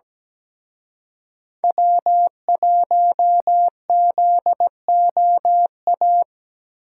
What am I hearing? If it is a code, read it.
Morse code, heard as a keyed tone, about 19 wpm, 700 Hz. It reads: W1ZOA
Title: Speed Racing - USA 1x3 Call Signs 17wpm